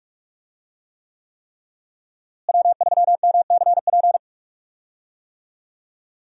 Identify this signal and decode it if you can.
Morse code, heard as a keyed tone, about 46 wpm, 700 Hz. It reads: W3MCP
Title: USA Call Signs - 5 Characters - Repeated 3x - 45wpm